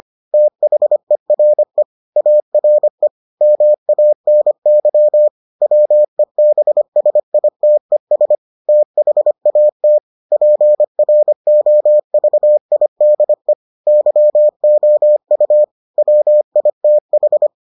Morse code, heard as a keyed tone, about 25 wpm, 600 Hz. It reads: THERE ARE MANY WEBSITES THAT PROVIDE YOU WITH